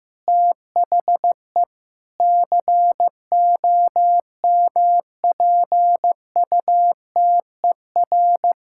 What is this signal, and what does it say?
Morse code, heard as a keyed tone, about 16 wpm, 700 Hz. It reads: THE COMPUTER
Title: Sentences from Top 1000 Words - Review Part 6 - 15wpm